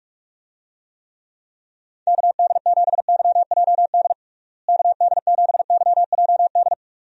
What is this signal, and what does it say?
Morse code, heard as a keyed tone, about 42 wpm, 700 Hz. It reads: KD7YJD KD7YJD
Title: USA Call Signs - 6 Characters - Repeated 2x - 45wpm